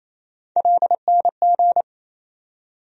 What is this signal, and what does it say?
Morse code, heard as a keyed tone, about 28 wpm, 700 Hz. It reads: LNG
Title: USA Call Sign Postfixes - 3 Letters 28wpm